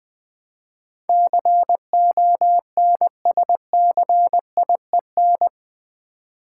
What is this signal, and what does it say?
Morse code, heard as a keyed tone, about 20 wpm, 700 Hz. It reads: CONSCIEN
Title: Build A Word - 4 Letters At A Time - 10-Letter Words - 20wpm